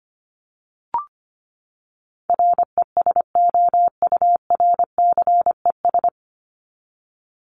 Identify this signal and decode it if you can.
Morse code, heard as a keyed tone, about 25 wpm, 700 Hz. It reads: RESOURCES
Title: ES:S Suffix - Top 100 Words with Base Words 25wpm